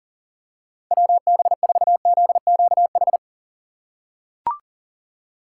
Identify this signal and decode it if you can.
Morse code, heard as a keyed tone, about 39 wpm, 700 Hz. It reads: WB4ZQH
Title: USA Call Signs - 6 Characters - 6x Character Spacing - 40wpm